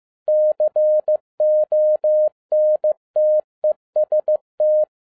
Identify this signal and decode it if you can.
Morse code, heard as a keyed tone, about 15 wpm, 610 Hz. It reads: CONTEST
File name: 2011-10-mp3 / n02-FIELD DAY CONTEST QRV FROM SAKAE VILLAGE 15wpm